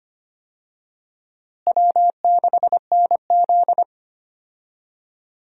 Morse code, heard as a keyed tone, about 25 wpm, 700 Hz. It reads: W6NZ
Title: USA Call Signs - 4 Characters - Repeated 2x - 25wpm